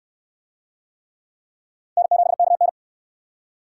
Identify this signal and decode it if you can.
Morse code, heard as a keyed tone, about 67 wpm, 700 Hz. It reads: N7KN